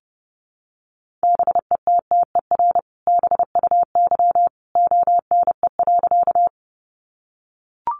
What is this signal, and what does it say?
Morse code, heard as a keyed tone, about 30 wpm, 700 Hz. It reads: BETTER BUY ONE.
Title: Sentences from Top 700 Words 30wpm